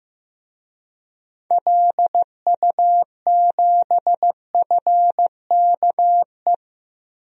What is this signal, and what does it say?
Morse code, heard as a keyed tone, about 15 wpm, 690 Hz. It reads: LU7FKE
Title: DXCC-100 Call Signs - 6 Characters - 15wpm